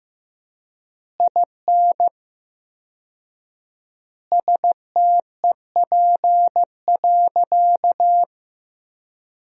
Morse code, heard as a keyed tone, about 15 wpm, 700 Hz. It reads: IN STEP.
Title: 4x Word-Spacing - Sentences from Top 400 Words - 15wpm